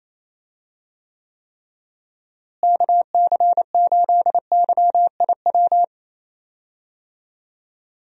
Morse code, heard as a keyed tone, about 28 wpm, 700 Hz. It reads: KC8YIW
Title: USA Call Signs - 6 Characters - Repeated 3x - 28wpm